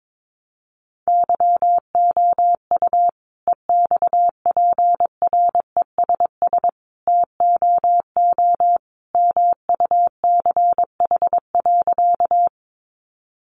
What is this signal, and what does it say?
Morse code, heard as a keyed tone, about 22 wpm, 700 Hz. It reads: YOU EXPRESS TOO MUCH.
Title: Sentences from Top 1100 Words 22wpm